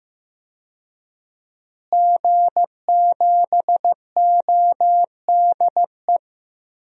Morse code, heard as a keyed tone, about 15 wpm, 700 Hz. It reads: G7ODE
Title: DXCC-100 Call Signs - 5 Characters - Repeated 2x - 15wpm